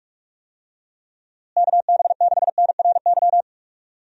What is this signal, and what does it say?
Morse code, heard as a keyed tone, about 46 wpm, 700 Hz. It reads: KB6NRY
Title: USA Call Signs - 6 Characters - Repeated 3x - 45wpm